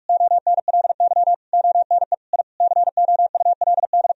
Morse code, heard as a keyed tone, about 41 wpm, 700 Hz. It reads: ONLY ONE I COULD
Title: Sentences from Top 1000 Words - Review Part 4 - 45wpm